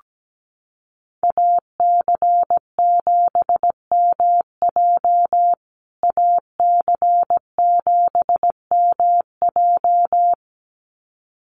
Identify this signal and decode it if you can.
Morse code, heard as a keyed tone, about 17 wpm, 700 Hz. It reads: AC7MJ AC7MJ
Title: USA Call Signs Repeated 2x 17wpm